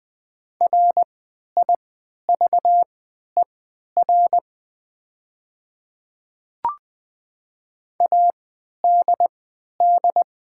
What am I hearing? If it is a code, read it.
Morse code, heard as a keyed tone, about 20 wpm, spaced as for 11 wpm, 700 Hz. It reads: RIVER ADD
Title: 3x Character Spacing - Top 300 Words - 20wpm